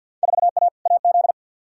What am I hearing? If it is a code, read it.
Morse code, heard as a keyed tone, about 47 wpm, 700 Hz. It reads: 4A AZ
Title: Field Day Exchanges - 50wpm